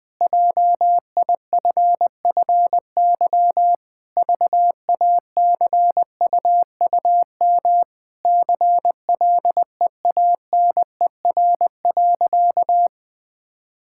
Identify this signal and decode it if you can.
Morse code, heard as a keyed tone, about 20 wpm, 700 Hz. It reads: JIFFY VACUUM CLEANER.
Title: Pangram Challenge v2 - 20wpm